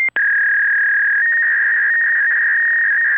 Fax machine handshake.